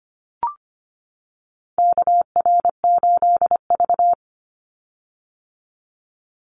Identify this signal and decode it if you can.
Morse code, heard as a keyed tone, about 25 wpm, 700 Hz. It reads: KR8V
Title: Call Signs - 25wpm